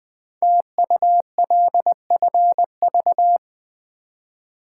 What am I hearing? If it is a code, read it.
Morse code, heard as a keyed tone, about 20 wpm, 700 Hz. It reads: TULFV
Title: Code Groups - 5 Letters - 20wpm